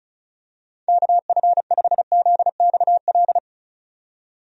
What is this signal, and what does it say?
Morse code, heard as a keyed tone, about 34 wpm, 700 Hz. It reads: KF5ZXL
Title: USA Call Signs - 6 Characters - 35wpm